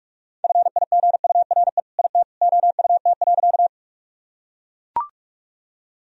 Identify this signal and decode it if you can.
Morse code, heard as a keyed tone, about 44 wpm, 700 Hz. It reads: FIGURE IT OUT.
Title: Sentences from Top 1000 Words - Review Part 2 - 45wpm